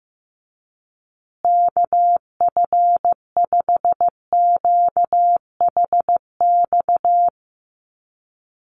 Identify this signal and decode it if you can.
Morse code, heard as a keyed tone, about 15 wpm, 700 Hz. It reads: KF5QHX
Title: USA Call Signs Repeated 2x 15wpm